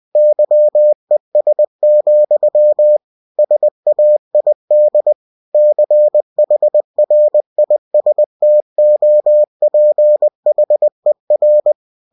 Morse code, heard as a keyed tone, about 20 wpm, 600 Hz. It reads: YES, SAID CHRISTOPHER
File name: cw-books / winnie-the-pooh / winnie-20wpm-page-0096